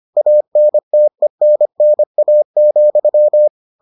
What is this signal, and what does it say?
Morse code, heard as a keyed tone, about 25 wpm, 600 Hz. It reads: ANTENNA,